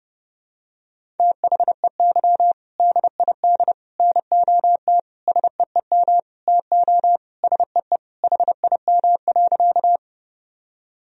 Morse code, heard as a keyed tone, about 30 wpm, 700 Hz. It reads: THEY DID NOT SEEM TO SEE HIM.
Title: Sentences from Top 1000 Words - Review Part 1 - 30wpm